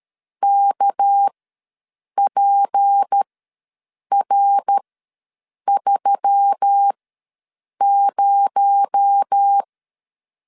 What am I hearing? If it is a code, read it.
Morse code, heard as a keyed tone, about 13 wpm, spaced as for 7 wpm, 800 Hz. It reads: KPR30
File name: LEZIONI / 6 PKX0 / 6 PKX0 ws 12wpm 3448